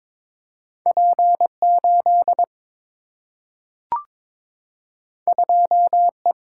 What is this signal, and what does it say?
Morse code, heard as a keyed tone, about 22 wpm, 700 Hz. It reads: P 8 2 E
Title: Single-Letter Single-Number 22wpm